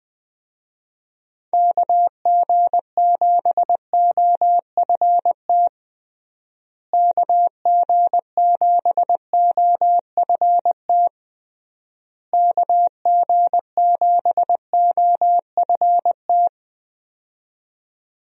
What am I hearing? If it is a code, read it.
Morse code, heard as a keyed tone, about 20 wpm, 700 Hz. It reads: KG7OFT KG7OFT KG7OFT
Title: USA Call Signs - 6 Characters - Repeated 3x - 20wpm